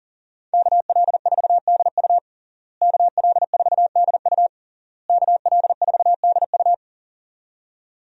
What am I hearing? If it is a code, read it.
Morse code, heard as a keyed tone, about 37 wpm, 700 Hz. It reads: KL4DU KL4DU KL4DU
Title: USA Call Signs - 5 Characters - Repeated 3x - 40wpm